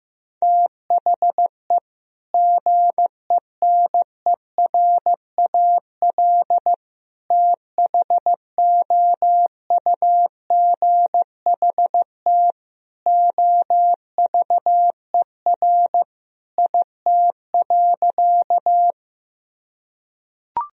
Morse code, heard as a keyed tone, about 15 wpm, 700 Hz. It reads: THE GENERAL THOUGHT OVER IT.
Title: Sentences from Top 1000 Words - Review Part 3 - 15wpm